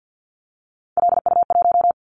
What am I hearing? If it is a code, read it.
Morse code, heard as a keyed tone, about 50 wpm, 700 Hz. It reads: LU1
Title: Callsign Prefixes for Top 100 DXCC - 50wpm